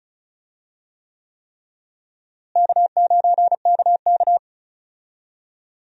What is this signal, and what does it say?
Morse code, heard as a keyed tone, about 34 wpm, 700 Hz. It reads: K9KK
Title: USA Call Signs - 4 Characters - Repeated 3x - 35wpm